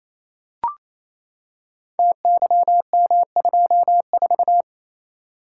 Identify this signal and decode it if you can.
Morse code, heard as a keyed tone, about 28 wpm, 700 Hz. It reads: TYM24
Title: Code Groups - 5 Letter:Numbers - 28wpm